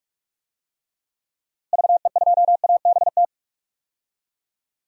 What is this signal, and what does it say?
Morse code, heard as a keyed tone, about 46 wpm, 700 Hz. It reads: VE2ABT
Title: DXCC-100 Call Signs - 6 Characters - Repeated 3x - 45wpm